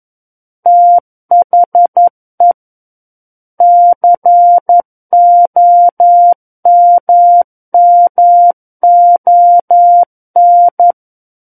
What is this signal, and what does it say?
Morse code, heard as a keyed tone, about 11 wpm, 700 Hz. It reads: THE COMMON
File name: 11 WPM